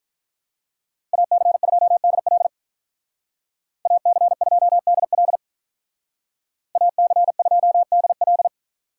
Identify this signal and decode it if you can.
Morse code, heard as a keyed tone, about 43 wpm, 700 Hz. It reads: AC2DL AC2DL AC2DL
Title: Speed Racing - USA Call Signs 35wpm